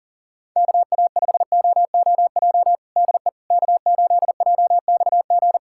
Morse code, heard as a keyed tone, about 40 wpm, 700 Hz. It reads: KA5OOJ DE K8JXG
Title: USA Callsign DE Callsign 40wpm